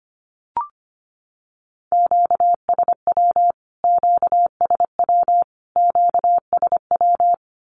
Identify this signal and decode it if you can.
Morse code, heard as a keyed tone, about 25 wpm, 700 Hz. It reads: QSW QSW QSW
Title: Q-Codes Repeated 3x 25wpm